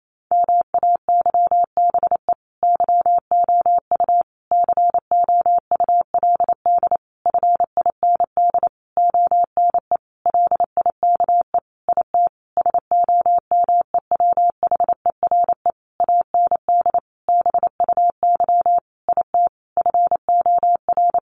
Morse code, heard as a keyed tone, about 28 wpm, 700 Hz. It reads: MAYBE YOU COULD FIND ONE LIKE IT SOMEWHERE AND BUY IT FOR
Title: Sentences from Top 1400 Words 28wpm